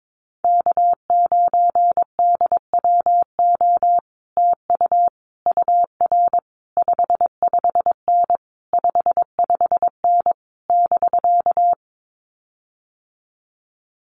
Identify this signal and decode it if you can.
Morse code, heard as a keyed tone, about 22 wpm, 700 Hz. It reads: K9DWO TU UR 55N 55N <BK>
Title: Common QSO Phrases Encore 22wpm